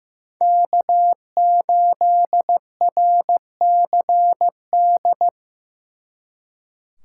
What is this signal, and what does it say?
Morse code, heard as a keyed tone, about 18 wpm, 700 Hz. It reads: K8RCD
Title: Speed Racing - USA 1x3 Call Signs 15wpm